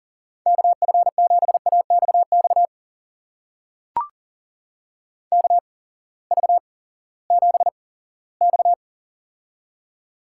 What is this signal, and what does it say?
Morse code, heard as a keyed tone, about 40 wpm, spaced as for 20 wpm, 700 Hz. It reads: KF7AXX KV7X
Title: USA Call Signs - 8x Character Spacing - 40wpm